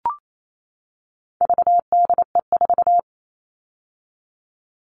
Morse code, heard as a keyed tone, about 28 wpm, 700 Hz. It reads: VDE4
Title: All Cut Numbers - 4-Digit 28wpm